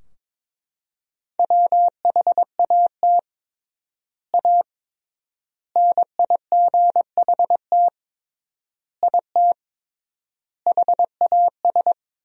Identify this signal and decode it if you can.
Morse code, heard as a keyed tone, about 22 wpm, 700 Hz. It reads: WHAT A NIGHT IT HAS
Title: 3x Word-Spacing - Sentences from Top 300 Words - 22wpm